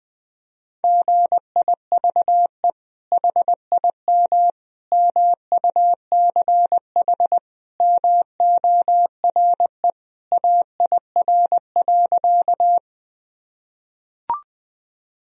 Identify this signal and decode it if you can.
Morse code, heard as a keyed tone, about 20 wpm, 700 Hz. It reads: GIVE HIM MUCH MORE AIR.
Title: Sentences from Top 1000 Words - Review Part 1 - 20wpm